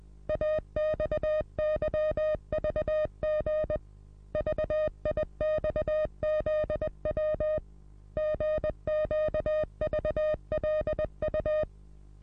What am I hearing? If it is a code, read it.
Morse code, heard as a keyed tone, about 20 wpm, 620 Hz. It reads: AXYVG VIXZW GQVLU
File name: repos / Morsekurs / DJ4UF Morsen27